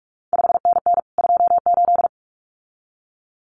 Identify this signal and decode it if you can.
Morse code, heard as a keyed tone, about 46 wpm, 700 Hz. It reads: 5NN 28
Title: CQ WW DX Contest Exchange Encore - 45wpm